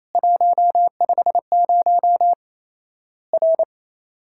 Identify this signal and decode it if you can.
Morse code, heard as a keyed tone, about 28 wpm, 700 Hz. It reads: 150 R
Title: Worked All Europe DX Contest - QTC Exchange - 28wpm